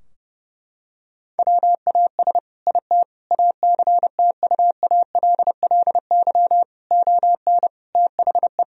Morse code, heard as a keyed tone, about 30 wpm, 700 Hz. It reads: WAS IT ACTUALLY ON THE
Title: Sentences from Top 1000 Words - Review Part 4 - 30wpm